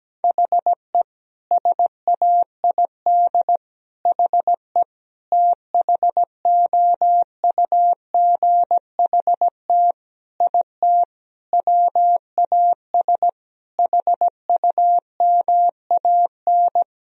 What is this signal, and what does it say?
Morse code, heard as a keyed tone, about 17 wpm, 700 Hz. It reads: HE SAID HE THOUGHT IT WAS HUMAN
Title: Sentences from Top 1000 Words - Review Part 3 - 17wpm